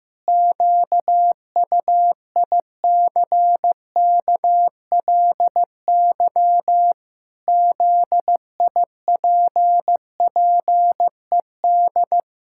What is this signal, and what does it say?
Morse code, heard as a keyed tone, about 15 wpm, 700 Hz. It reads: QUICKLY ZIPPED
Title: Pangram Challenge v2 - 15wpm